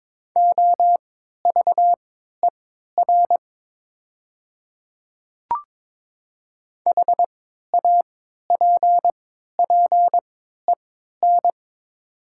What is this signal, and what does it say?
Morse code, heard as a keyed tone, about 22 wpm, spaced as for 13 wpm, 700 Hz. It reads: OVER HAPPEN
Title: 3x Character Spacing - Top 500 Words - 22wpm